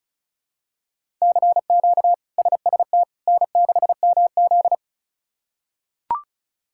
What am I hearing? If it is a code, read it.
Morse code, heard as a keyed tone, about 36 wpm, 700 Hz. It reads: CQ SST N6MZ
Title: K1USN SST QSO Practice - 35wpm